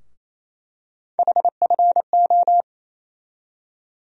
Morse code, heard as a keyed tone, about 28 wpm, 700 Hz. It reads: HFO
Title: USA Call Sign Postfixes - 3 Letters 28wpm